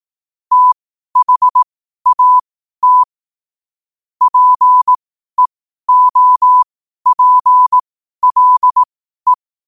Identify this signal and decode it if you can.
Morse code, heard as a keyed tone, about 18 wpm, spaced as for 12 wpm, 1000 Hz. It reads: THAT PEOPLE
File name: qotdfiles / 12wpm